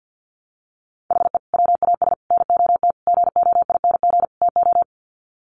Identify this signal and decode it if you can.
Morse code, heard as a keyed tone, about 48 wpm, 700 Hz. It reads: HE WAS NOT GOING TO
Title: Sets of 5 Words 50wpm